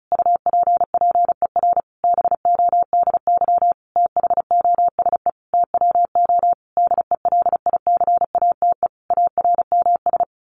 Morse code, heard as a keyed tone, about 34 wpm, 700 Hz. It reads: UPPER BODY THOSE TWO DELICATE ARMS
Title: A Sound Of Thunder 35wpm